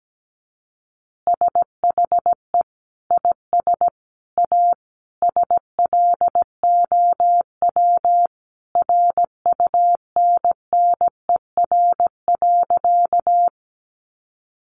Morse code, heard as a keyed tone, about 17 wpm, 700 Hz. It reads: SHE IS A SLOW RUNNER.
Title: Additional Easy Phrases 17wpm